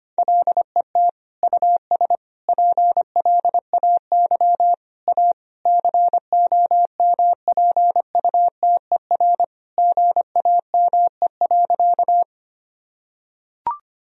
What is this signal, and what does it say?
Morse code, heard as a keyed tone, about 25 wpm, 700 Hz. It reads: LET US PLAY A COMPUTER GAME.
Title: Sentences from Top 1000 Words - Review Part 3 - 25wpm